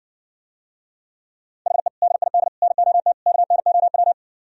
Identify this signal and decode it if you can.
Morse code, heard as a keyed tone, about 62 wpm, 700 Hz. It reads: HE DID NOT KNOW